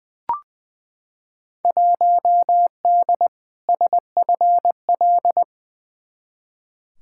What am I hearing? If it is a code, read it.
Morse code, heard as a keyed tone, about 20 wpm, 700 Hz. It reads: 1D SFL
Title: Field Day Exchanges - 20wpm